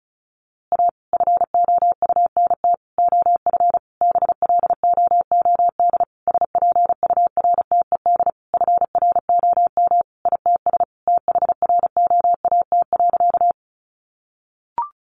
Morse code, heard as a keyed tone, about 34 wpm, 700 Hz. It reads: A FOUNT OF BLOOD SPURTED FROM ITS THROAT.
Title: A Sound Of Thunder 35wpm